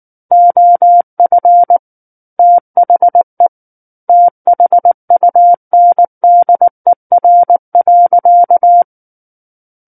Morse code, heard as a keyed tone, about 19 wpm, 700 Hz. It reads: OF THE THUNDER.
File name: Book One / 10-Chapter 10 - 19 WPM